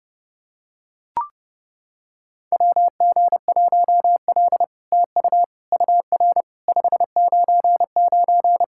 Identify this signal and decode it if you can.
Morse code, heard as a keyed tone, about 30 wpm, 700 Hz. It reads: WG1L TU UR 599
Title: SOTA QSO Practice 30wpm